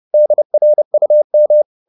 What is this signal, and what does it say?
Morse code, heard as a keyed tone, about 30 wpm, 600 Hz. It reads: DRUM
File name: the old man and the sea / the old man and the sea 1 30wpm